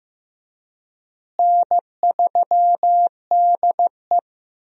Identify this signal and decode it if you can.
Morse code, heard as a keyed tone, about 15 wpm, 700 Hz. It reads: N3DE
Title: USA Call Signs - 4 Characters - Repeated 3x - 15wpm